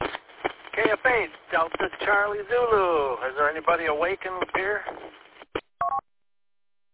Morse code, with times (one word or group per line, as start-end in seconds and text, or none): none